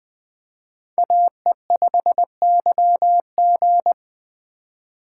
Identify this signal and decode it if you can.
Morse code, heard as a keyed tone, about 24 wpm, 700 Hz. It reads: AE5YG
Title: Speed Racing - USA 2x2 Call Signs 20wpm